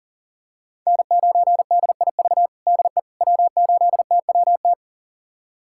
Morse code, heard as a keyed tone, about 40 wpm, 700 Hz. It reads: N9DIV DE W8TWT
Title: USA Callsign DE Callsign - 5 Characters - 40wpm